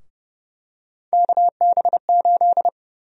Morse code, heard as a keyed tone, about 31 wpm, 700 Hz. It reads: KB8
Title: USA Call Sign Prefixes - Repeated 2x - 30wpm